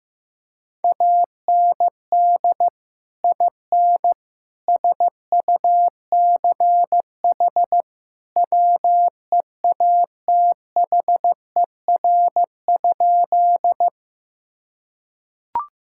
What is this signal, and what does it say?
Morse code, heard as a keyed tone, about 16 wpm, 700 Hz. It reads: AND IN SUCH WEATHER?
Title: Sentences from Top 1000 Words - Review Part 5 - 15wpm